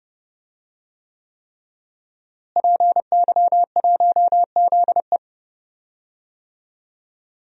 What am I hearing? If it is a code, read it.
Morse code, heard as a keyed tone, about 32 wpm, 700 Hz. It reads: PY1ZE
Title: DXCC-100 Call Signs - 5 Characters - Repeated 3x - 30wpm